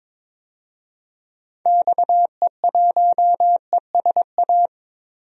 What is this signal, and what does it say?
Morse code, heard as a keyed tone, about 22 wpm, 700 Hz. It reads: XE1ESA
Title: DXCC-100 Call Signs - 6 Characters - Repeated 2x - 22wpm